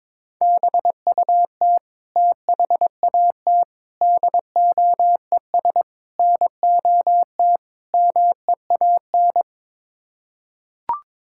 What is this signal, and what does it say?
Morse code, heard as a keyed tone, about 22 wpm, 700 Hz. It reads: BUT THAT DOES NOT MEAN
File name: practice / 22wpm / Build A Phrase - 5-Word Phrases - 22wpm